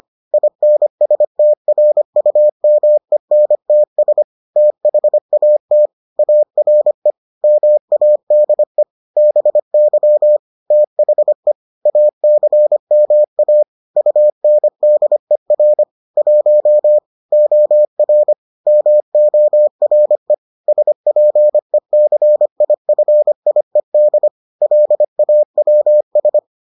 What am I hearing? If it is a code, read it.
Morse code, heard as a keyed tone, about 25 wpm, 600 Hz. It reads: INSTRUMENTS THAT ARE MADE BY THE ACMA UNDER 1 OR MORE SPECIFIED LAWS